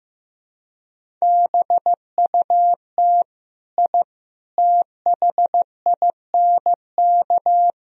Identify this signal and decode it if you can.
Morse code, heard as a keyed tone, about 16 wpm, 700 Hz. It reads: BUT I THINK
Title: Speed-Racing - Sets of 4 Words 15wpm